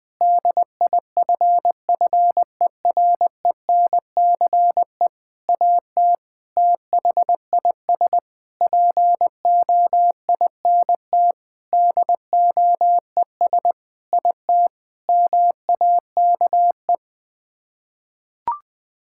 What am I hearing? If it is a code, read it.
Morse code, heard as a keyed tone, about 20 wpm, 700 Hz. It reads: DIFFERENCE AT THIS POINT DOES IT MAKE
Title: Sets of 8 Words - 20wpm